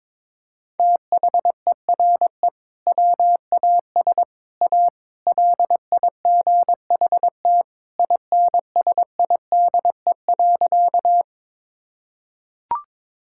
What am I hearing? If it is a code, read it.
Morse code, heard as a keyed tone, about 21 wpm, 700 Hz. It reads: THERE WAS A LIGHT INSIDE.
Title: Sentences from Top 1000 Words - Review Part 6 - 22wpm